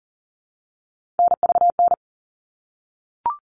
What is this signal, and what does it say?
Morse code, heard as a keyed tone, about 40 wpm, 700 Hz. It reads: NVN
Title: All Cut Numbers - 3-Digit 40wpm